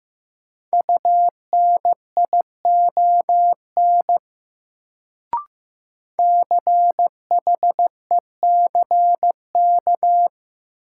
Morse code, heard as a keyed tone, about 15 wpm, 700 Hz. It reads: UNION CHECK
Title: Sentences from Top 1400 Words - New Word Drill Alt - 15wpm